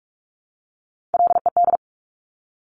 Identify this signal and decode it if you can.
Morse code, heard as a keyed tone, about 46 wpm, 700 Hz. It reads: LED
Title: Top Three Letter Combinations in Words 45wpm